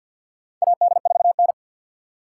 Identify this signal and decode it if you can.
Morse code, heard as a keyed tone, about 48 wpm, 700 Hz. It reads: AD4N
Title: USA Call Signs - 4 Characters - 6x Character Spacing - 50wpm